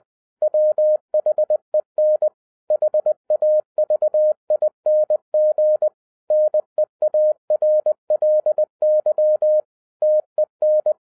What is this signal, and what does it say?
Morse code, heard as a keyed tone, about 20 wpm, 610 Hz. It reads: WHEN HAVING NEARLY TEN
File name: 2012-01-mp3 / n16-THROUGH EIGHT MONTHS-20wpm